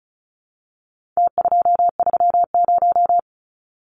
Morse code, heard as a keyed tone, about 34 wpm, 700 Hz. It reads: T230
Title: Numbers - Contest Style - 4-Digit 35wpm